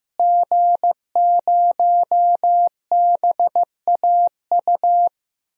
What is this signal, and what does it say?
Morse code, heard as a keyed tone, about 15 wpm, 700 Hz. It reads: G0BAU
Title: DXCC-100 Call Signs - 5 Characters - Repeated 3x - 15wpm